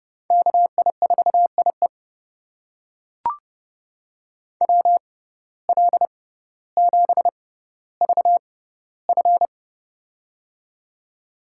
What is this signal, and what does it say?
Morse code, heard as a keyed tone, about 30 wpm, spaced as for 15 wpm, 700 Hz. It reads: KI4IE WL7VF
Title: USA Call Signs - 5 Characters - 6x Character Spacing - 30wpm